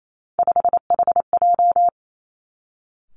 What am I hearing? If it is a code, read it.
Morse code, heard as a keyed tone, about 28 wpm, 700 Hz. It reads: 5HJ
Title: Three Challenging Letter-Number Combinations 28wpm